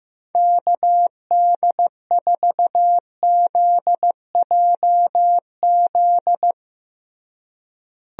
Morse code, heard as a keyed tone, about 15 wpm, 700 Hz. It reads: KD4ZJZ
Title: USA Call Signs - 6 Characters - Repeated 2x - 15wpm